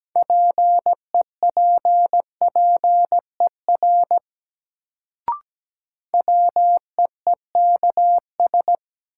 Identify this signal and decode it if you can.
Morse code, heard as a keyed tone, about 17 wpm, 700 Hz. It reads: PEPPER WEEKS
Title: Sets of 5 Words - New Word Drill - 17wpm